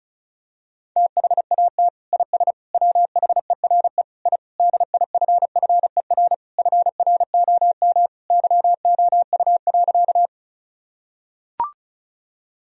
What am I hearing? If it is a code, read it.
Morse code, heard as a keyed tone, about 36 wpm, 700 Hz. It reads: THAT IS WHERE I DIFFER FROM YOU.
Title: Sentences from Top 1000 Words - Review Part 1 - 35wpm